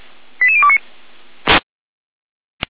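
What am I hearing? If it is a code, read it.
Five-tone selective call (ZVEI), heard as quick runs of five tones, tones of about 80 ms. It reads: 88029
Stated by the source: ZVEI five-tone selective call received by SDR, recorded on the audio server of Feuerwehr Orlamünde.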